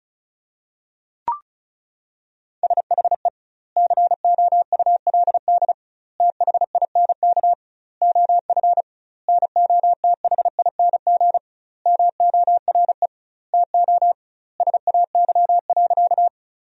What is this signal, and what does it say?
Morse code, heard as a keyed tone, about 34 wpm, 700 Hz. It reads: SHE COULD THINK OF NOTHING MORE TO SAY.
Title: 2x Word-Spacing - Sentences from Top 500 Words - 35wpm